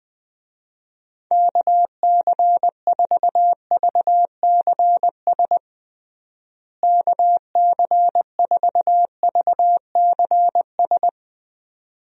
Morse code, heard as a keyed tone, about 20 wpm, 700 Hz. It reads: KC4VCS KC4VCS
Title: USA Call Signs - 6 Characters - Repeated 2x - 20wpm